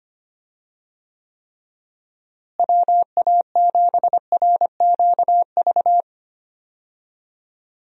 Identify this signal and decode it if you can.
Morse code, heard as a keyed tone, about 25 wpm, 700 Hz. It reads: WA7RQV
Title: USA Call Signs - 6 Characters - Repeated 3x - 25wpm